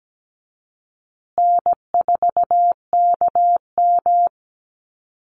Morse code, heard as a keyed tone, about 17 wpm, 700 Hz. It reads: N4KM
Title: Call Signs Repeated 2x - 17wpm